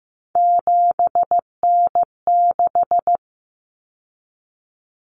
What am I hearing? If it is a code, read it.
Morse code, heard as a keyed tone, about 15 wpm, 700 Hz. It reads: 7N6
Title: Common Cut Numbers - 3-Digit 15wpm